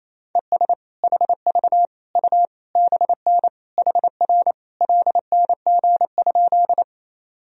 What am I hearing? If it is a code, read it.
Morse code, heard as a keyed tone, about 28 wpm, 700 Hz. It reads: ES HV U BN HR LNG?
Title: Sentences from Top 200 Words and Abbreviations 28wpm